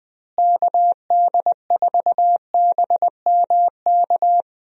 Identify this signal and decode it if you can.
Morse code, heard as a keyed tone, about 21 wpm, 700 Hz. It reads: KD4BMK
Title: USA Call Signs - 6 Characters - Repeated 3x - 20wpm